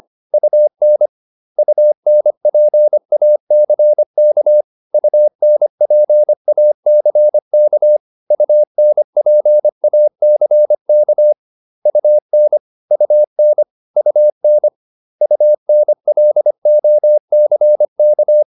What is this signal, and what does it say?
Morse code, heard as a keyed tone, about 25 wpm, 600 Hz. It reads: UN UNPACK UNPACK UNPACK UN UN UN UNLOCK